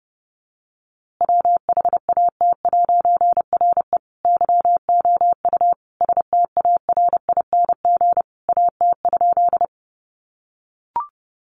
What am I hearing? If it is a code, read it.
Morse code, heard as a keyed tone, about 30 wpm, 700 Hz. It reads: WHAT'RE YOU STARING AT?
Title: A Sound Of Thunder 30wpm